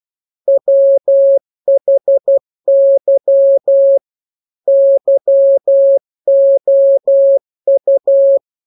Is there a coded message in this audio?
Yes: Morse code from 0.5 to 8.4 s, heard as a keyed tone, about 12 wpm, 550 Hz. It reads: WHY YOU